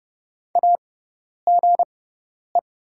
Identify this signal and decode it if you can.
Morse code, heard as a keyed tone, about 30 wpm, spaced as for 10 wpm, 700 Hz. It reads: AGE
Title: 6x Character Spacing - Top 500 Words - 30wpm